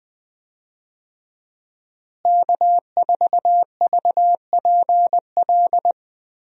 Morse code, heard as a keyed tone, about 21 wpm, 700 Hz. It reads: K4VPL
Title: USA Call Signs - 5 Characters - Repeated 3x - 20wpm